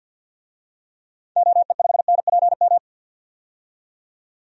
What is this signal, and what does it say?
Morse code, heard as a keyed tone, about 49 wpm, 700 Hz. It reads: OE5NPM
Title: DXCC-100 Call Signs - 6 Characters - 50wpm